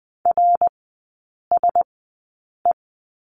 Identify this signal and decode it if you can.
Morse code, heard as a keyed tone, about 20 wpm, spaced as for 8 wpm, 700 Hz. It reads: RSE
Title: 053 Lesson 10e - Letter r - Three Characters 20wpm